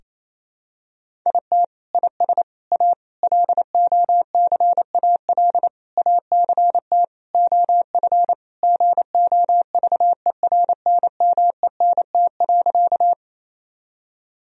Morse code, heard as a keyed tone, about 28 wpm, 700 Hz. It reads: IT IS A LOCAL ACT OF GOVERNMENT.
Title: Sentences from Top 1000 Words - Review Part 3 - 28wpm